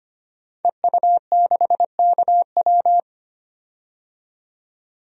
Morse code, heard as a keyed tone, about 25 wpm, 690 Hz. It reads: EU6KW
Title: DXCC-100 Call Signs - 5 Characters - 25wpm